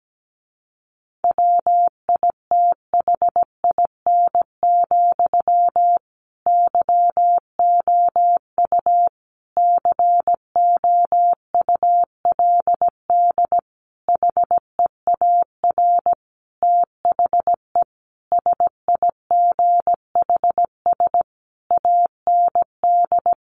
Morse code, heard as a keyed tone, about 17 wpm, 700 Hz. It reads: WITHIN, YOU COULD HEAR THE SIGHS AND
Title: A Sound Of Thunder 17wpm